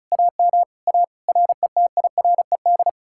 Morse code, heard as a keyed tone, about 40 wpm, 700 Hz. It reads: AM A RETIRED